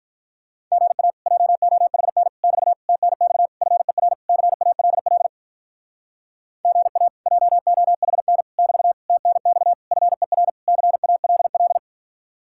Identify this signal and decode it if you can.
Morse code, heard as a keyed tone, about 43 wpm, 700 Hz. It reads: GA JOHN <BT> TNX FER CALL GA JOHN <BT> TNX FER CALL
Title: Speed-Racing Common QSO Phrases Encore - 35wpm